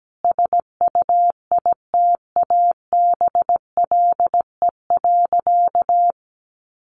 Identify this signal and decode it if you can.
Morse code, heard as a keyed tone, about 17 wpm, 700 Hz. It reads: SUITABLE.
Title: Sentences from Top 1400 Words 17wpm